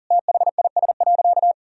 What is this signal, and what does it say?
Morse code, heard as a keyed tone, about 40 wpm, 700 Hz. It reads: THIS.
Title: Sentences from Top 1000 Words - Review Part 4 - 40wpm